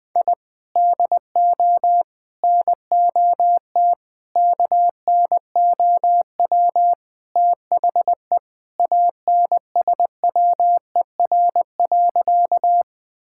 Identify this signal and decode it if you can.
Morse code, heard as a keyed tone, about 21 wpm, 700 Hz. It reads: I DO NOT KNOW THE ANSWER.
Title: Sentences with 5-Word Phrases - 20wpm